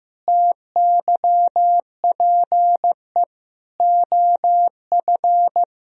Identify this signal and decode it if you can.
Morse code, heard as a keyed tone, about 16 wpm, 700 Hz. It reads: TYPE OF
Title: Sentences from Top 1000 Words - Review Part 3 - 15wpm